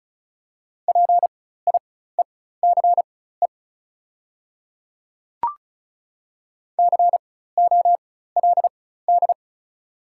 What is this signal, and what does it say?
Morse code, heard as a keyed tone, about 34 wpm, spaced as for 16 wpm, 690 Hz. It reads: PIECE COLD
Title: 4x Character Spacing - Top 500 Words - 35wpm